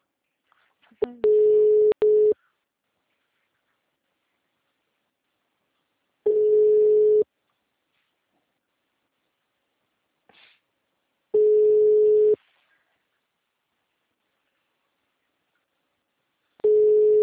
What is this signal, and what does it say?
Ringback tone (Europe, 425 Hz).
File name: out- -2002- - - .10028